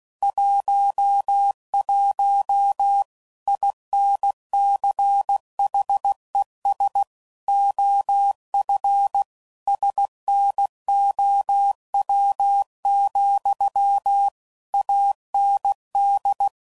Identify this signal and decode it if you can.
Morse code, heard as a keyed tone, about 16 wpm, 800 Hz. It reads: 11 INCHES OF SNOW, AND
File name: CW Code Groups @15WPM / QSOs / WA7QOX